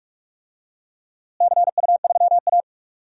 Morse code, heard as a keyed tone, about 44 wpm, 700 Hz. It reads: CU3A
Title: DXCC-100 Call Signs - 4 Characters - Repeated 2x - 45wpm